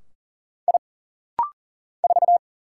Morse code, heard as a keyed tone, about 40 wpm, 700 Hz. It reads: I 4
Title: Single Letter-Number Alt - Rapid-Fire 40wpm